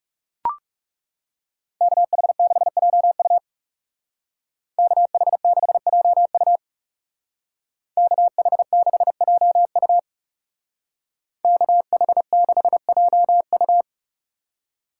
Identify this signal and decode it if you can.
Morse code, heard as a keyed tone, about 36 wpm, 700 Hz. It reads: KH6JU KH6JU KH6JU KH6JU
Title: Speed Racing - USA 2x2 Call Signs 30wpm